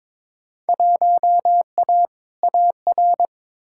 Morse code, heard as a keyed tone, about 22 wpm, 700 Hz. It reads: 1A AR
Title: Field Day Exchanges Encore - 22wpm